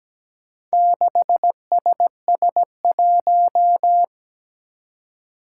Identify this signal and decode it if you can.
Morse code, heard as a keyed tone, about 17 wpm, 700 Hz. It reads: 6SS1
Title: Four Challenging Letter-Number Combinations 17wpm